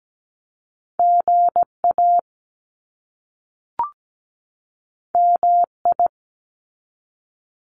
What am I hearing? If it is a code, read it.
Morse code, heard as a keyed tone, about 17 wpm, 700 Hz. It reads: GA MI
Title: US States 17wpm